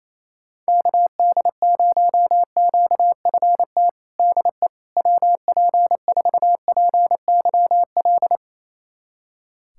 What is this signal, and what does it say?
Morse code, heard as a keyed tone, about 27 wpm, 700 Hz. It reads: KD0QFT DE WP4PYL
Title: USA Callsign DE Callsign - 6 Characters - 28wpm